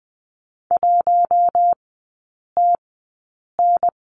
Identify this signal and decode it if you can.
Morse code, heard as a keyed tone, about 20 wpm, spaced as for 8 wpm, 700 Hz. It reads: 1TN
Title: 074 Lesson 13e - Letter L - Three Characters 20wpm